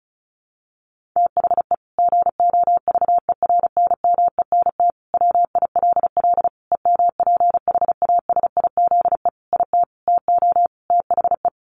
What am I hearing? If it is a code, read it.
Morse code, heard as a keyed tone, about 34 wpm, 700 Hz. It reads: THE GOVERNMENT WILL EMPHASIZE IT TO THE
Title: Sentences from Top 1500 Words 35wpm